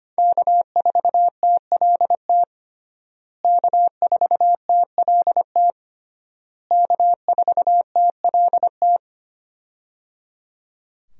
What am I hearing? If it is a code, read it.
Morse code, heard as a keyed tone, about 25 wpm, 700 Hz. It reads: K4TLT K4TLT K4TLT
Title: USA Call Signs - 5 Characters - Repeated 3x - 25wpm